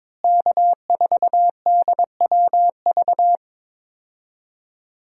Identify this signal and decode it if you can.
Morse code, heard as a keyed tone, about 22 wpm, 700 Hz. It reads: K4DWV
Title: USA Call Signs - 5 Characters - Repeated 2x - 22wpm